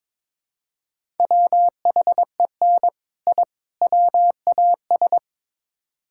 Morse code, heard as a keyed tone, about 22 wpm, 700 Hz. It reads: WHEN I WAS
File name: practice / 22wpm / Build A Phrase - 5-Word Phrases - 22wpm